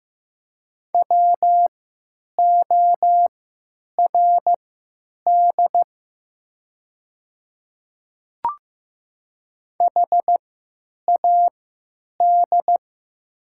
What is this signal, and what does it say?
Morse code, heard as a keyed tone, about 15 wpm, spaced as for 9 wpm, 700 Hz. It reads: WORD HAD
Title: 3x Character Spacing - Top 200 Words - 15wpm